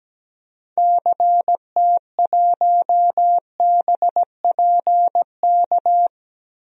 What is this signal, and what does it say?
Morse code, heard as a keyed tone, about 17 wpm, 700 Hz. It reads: CT1BPK
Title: DXCC-100 Call Signs - 6 Characters - Repeated 3x - 17wpm